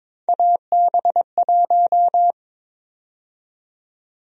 Morse code, heard as a keyed tone, about 22 wpm, 700 Hz. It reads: AB1
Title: USA Call Sign Prefixes - 22wpm